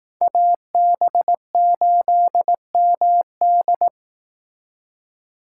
Morse code, heard as a keyed tone, about 18 wpm, 700 Hz. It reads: AB8MD
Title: Speed Racing - USA 2x2 Call Signs 15wpm